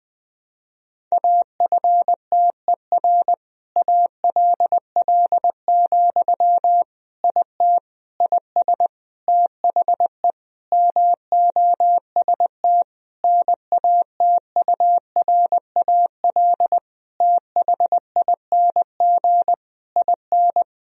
Morse code, heard as a keyed tone, about 20 wpm, 700 Hz. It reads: AFTER ALL, IT IS THE MOST NATURAL THING IN
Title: Sentences from Top 1000 Words - Review Part 4 - 20wpm